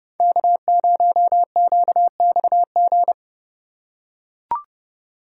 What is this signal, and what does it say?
Morse code, heard as a keyed tone, about 30 wpm, 690 Hz. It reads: K0QXG
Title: USA Call Signs - 3x Character Spacing - 30wpm